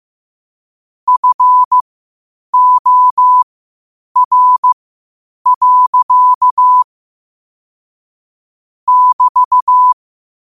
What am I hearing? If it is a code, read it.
Morse code, heard as a keyed tone, about 15 wpm, spaced as for 8 wpm, 1000 Hz. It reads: FOR. <BT>
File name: qotdfiles / 8wpm